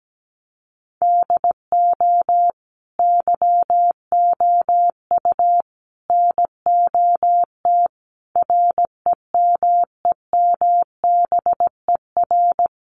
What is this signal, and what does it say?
Morse code, heard as a keyed tone, about 17 wpm, 700 Hz. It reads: DO YOU NOT REMEMBER
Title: Sentences from Top 600 Words 17wpm